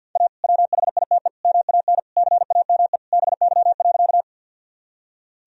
Morse code, heard as a keyed tone, about 47 wpm, 700 Hz. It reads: A WHITE MAN CAME BY.
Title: Sentences from Top 1000 Words - Review Part 1 - 50wpm